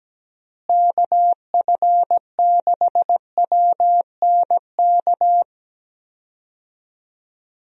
Morse code, heard as a keyed tone, about 17 wpm, 700 Hz. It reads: KF6WNK
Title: USA Call Signs - 6 Characters - Repeated 3x - 17wpm